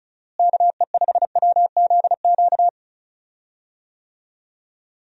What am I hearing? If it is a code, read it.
Morse code, heard as a keyed tone, about 34 wpm, 700 Hz. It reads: KE5WZQ
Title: USA Call Signs - 6 Characters - Repeated 2x - 35wpm